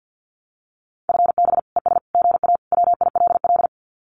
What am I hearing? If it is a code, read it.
Morse code, heard as a keyed tone, about 48 wpm, 700 Hz. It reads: FB ES GA WILL
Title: Common QSO Phrases Encore 50wpm